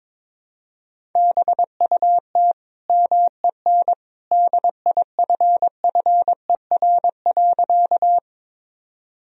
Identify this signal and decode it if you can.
Morse code, heard as a keyed tone, about 22 wpm, 700 Hz. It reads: BUT MEN DIFFER.
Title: Sentences from Top 1000 Words - Review Part 1 - 22wpm